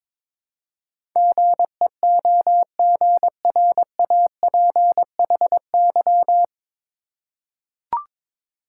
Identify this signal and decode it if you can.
Morse code, heard as a keyed tone, about 22 wpm, 700 Hz. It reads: GEOGRAPHY
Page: Morse Code Ninja practice 22wpm